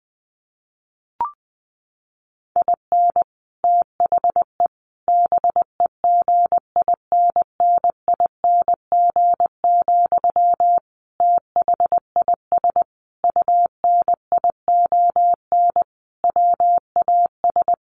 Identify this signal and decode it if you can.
Morse code, heard as a keyed tone, about 20 wpm, 700 Hz. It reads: IN THE BEGINNING, THIS UNION WAS
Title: Sentences from Top 1400 Words 20wpm